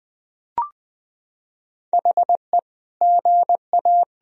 Morse code, heard as a keyed tone, about 21 wpm, 700 Hz. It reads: HE GA
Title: Sentences from Top 1000 Words - Review Part 1 - 20wpm